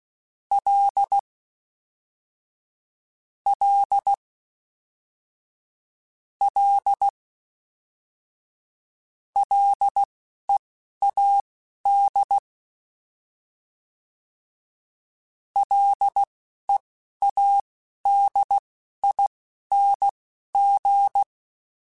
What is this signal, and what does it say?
Morse code, heard as a keyed tone, about 16 wpm, spaced as for 11 wpm, 800 Hz. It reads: L L L LEAD LEADING